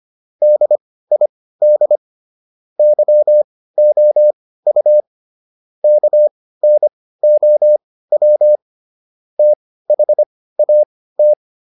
Morse code, heard as a keyed tone, about 25 wpm, spaced as for 16 wpm, 600 Hz. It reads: DID YOU KNOW THAT